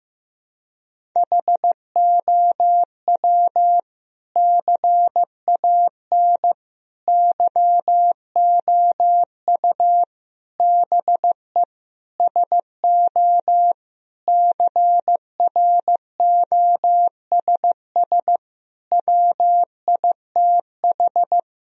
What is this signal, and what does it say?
Morse code, heard as a keyed tone, about 15 wpm, 700 Hz. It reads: HOW CAN YOU BE SO CROSS WITH
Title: Sentences from Top 1000 Words - Review Part 1 - 15wpm